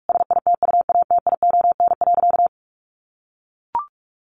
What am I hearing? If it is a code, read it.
Morse code, heard as a keyed tone, about 44 wpm, 700 Hz. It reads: SITUATION.
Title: Sentences from Top 1200 Words 45wpm